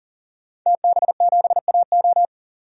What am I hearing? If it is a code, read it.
Morse code, heard as a keyed tone, about 40 wpm, 700 Hz. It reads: TB7AO
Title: DXCC-100 Call Signs - 5 Characters - 40wpm